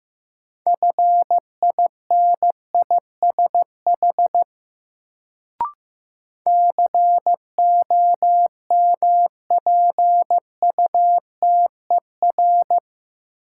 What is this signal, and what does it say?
Morse code, heard as a keyed tone, about 15 wpm, 700 Hz. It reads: FINISH COMPUTER
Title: Sentences from Top 1000 Words - Review Part 3 - New Word Drill - 15wpm